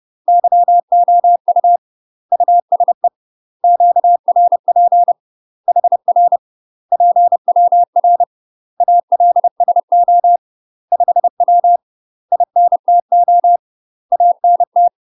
Morse code, heard as a keyed tone, about 30 wpm, 700 Hz. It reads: YOU USE QRP HR PWR ALSO 5W INTO ANT
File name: UK MASTER QSOs TRAINING / 30WPM / UK MASTER QSOs 0044